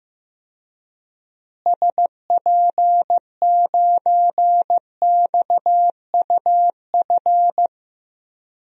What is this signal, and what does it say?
Morse code, heard as a keyed tone, about 15 wpm, 700 Hz. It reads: SP9XUF
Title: DXCC-100 Call Signs - 6 Characters - Repeated 2x - 15wpm